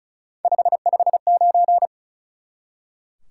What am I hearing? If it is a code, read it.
Morse code, heard as a keyed tone, about 36 wpm, 700 Hz. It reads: 559
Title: Top 180 QSO Elements 35wpm